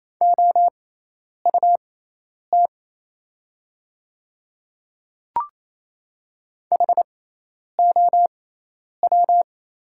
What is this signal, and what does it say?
Morse code, heard as a keyed tone, about 28 wpm, spaced as for 10 wpm, 700 Hz. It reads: OUT HOW
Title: 6x Character Spacing - Top 100 Words - 28wpm